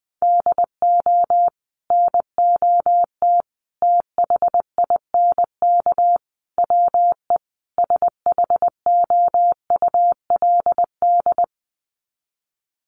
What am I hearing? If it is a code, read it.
Morse code, heard as a keyed tone, about 20 wpm, 700 Hz. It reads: DO NOT THINK WE SHOULD
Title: Sets of 5 Words Encore 20wpm